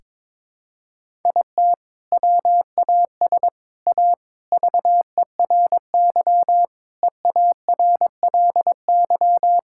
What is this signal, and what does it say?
Morse code, heard as a keyed tone, about 22 wpm, 700 Hz. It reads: IT WAS A VERY EARLY
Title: Sentences from Top 1000 Words - Review Part 6 - 22wpm